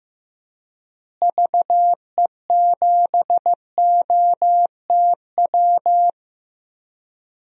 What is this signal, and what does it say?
Morse code, heard as a keyed tone, about 15 wpm, 700 Hz. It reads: VE7OTW
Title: DXCC-100 Call Signs - 6 Characters - Repeated 3x - 15wpm